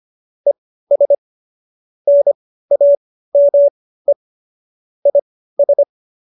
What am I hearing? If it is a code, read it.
Morse code, heard as a keyed tone, about 25 wpm, spaced as for 15 wpm, 570 Hz. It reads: ES NAME IS
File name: qso222 15